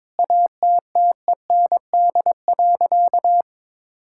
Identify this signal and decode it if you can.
Morse code, heard as a keyed tone, about 22 wpm, 700 Hz. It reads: ATTEND.
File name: Sentences with 7-Word Phrases - 22wpm